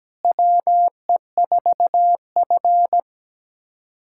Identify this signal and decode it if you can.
Morse code, heard as a keyed tone, about 17 wpm, 700 Hz. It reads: WE4F
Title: USA Callsign DE Callsign - 4 Characters - 17wpm